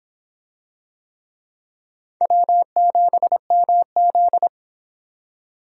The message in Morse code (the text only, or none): W7MZ